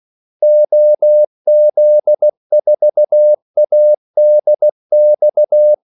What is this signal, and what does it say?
Morse code, heard as a keyed tone, about 16 wpm, 600 Hz. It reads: OZ4ADX
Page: Kaldesignaler 16 wpm